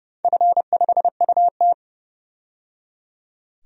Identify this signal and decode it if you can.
Morse code, heard as a keyed tone, about 29 wpm, 700 Hz. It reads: F5UT
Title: DXCC-100 Call Signs - 4 Characters - 30wpm